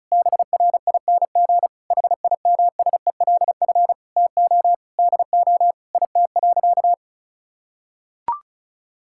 Morse code, heard as a keyed tone, about 36 wpm, 700 Hz. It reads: BRING HIMSELF TO DO IT.
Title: Sentences from Top 1600 Words 35wpm